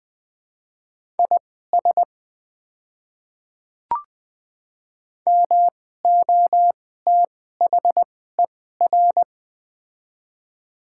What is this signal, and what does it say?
Morse code, heard as a keyed tone, about 21 wpm, spaced as for 15 wpm, 700 Hz. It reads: IS MOTHER
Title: 2x Character Spacing - Top 200 Words - 20wpm